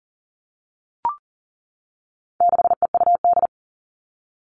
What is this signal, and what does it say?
Morse code, heard as a keyed tone, about 40 wpm, 700 Hz. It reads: 6EUD
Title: All Cut Numbers - 4-Digit 40wpm